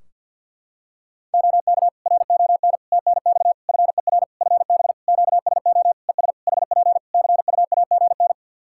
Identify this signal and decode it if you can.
Morse code, heard as a keyed tone, about 50 wpm, 700 Hz. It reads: OK RON TNX FER FB QSO ES HP CUAGN
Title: Practice QSO - Exchange 4 - Ending - 50wpm